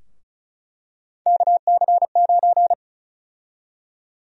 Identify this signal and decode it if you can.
Morse code, heard as a keyed tone, about 36 wpm, 700 Hz. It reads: KC9
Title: USA Call Sign Prefixes - Repeated 2x - 35wpm